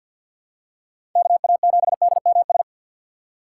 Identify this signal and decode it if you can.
Morse code, heard as a keyed tone, about 53 wpm, 700 Hz. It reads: KA7DMS
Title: USA Call Signs - 6 Characters - 6x Character Spacing - 50wpm